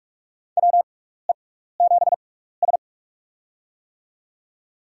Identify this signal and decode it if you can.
Morse code, heard as a keyed tone, about 46 wpm, spaced as for 16 wpm, 700 Hz. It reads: WE7S
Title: USA Call Signs - 4 Characters - 6x Character Spacing - 45wpm